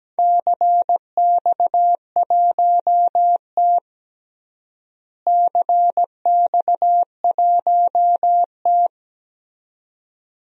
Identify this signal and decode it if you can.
Morse code, heard as a keyed tone, about 17 wpm, 700 Hz. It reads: CX1T CX1T
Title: DXCC-100 Call Signs - 4 Characters - Repeated 2x - 17wpm